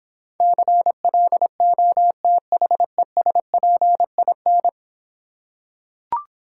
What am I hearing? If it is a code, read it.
Morse code, heard as a keyed tone, about 26 wpm, 700 Hz. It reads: CLOTHESPIN